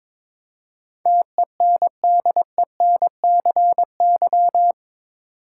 Morse code, heard as a keyed tone, about 21 wpm, 700 Hz. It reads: TENDENCY
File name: practice / 22wpm / Eight Letter Words - 22wpm